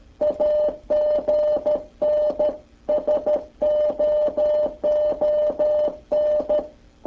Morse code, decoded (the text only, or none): AGN SOON